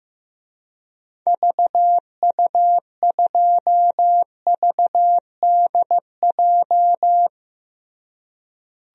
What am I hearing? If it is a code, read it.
Morse code, heard as a keyed tone, about 15 wpm, 700 Hz. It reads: VU2VDJ
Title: DXCC-100 Call Signs - 6 Characters - Repeated 2x - 15wpm